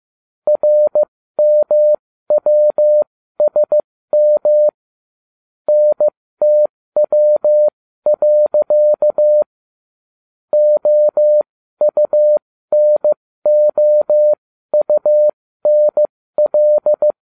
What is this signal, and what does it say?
Morse code, heard as a keyed tone, about 15 wpm, spaced as for 13 wpm, 600 Hz. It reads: RMWSM NTW. OUNOUNL